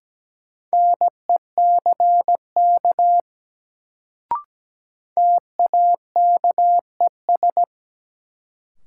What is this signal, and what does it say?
Morse code, heard as a keyed tone, about 17 wpm, 700 Hz. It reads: NECK TAKES
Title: Sets of 5 Words Encore - New Word Drill - 17wpm